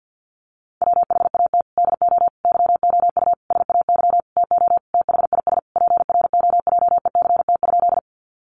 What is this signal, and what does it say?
Morse code, heard as a keyed tone, about 50 wpm, 700 Hz. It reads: WHAT DO YOU SAY TO THIS PROJECT?
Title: Sentences from Top 1400 Words 50wpm